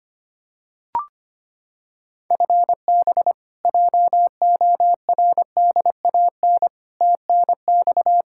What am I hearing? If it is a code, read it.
Morse code, heard as a keyed tone, about 25 wpm, 700 Hz. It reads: FB JORDAN TNX
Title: Practice QSO - Exchange 3 - More Info - 25wpm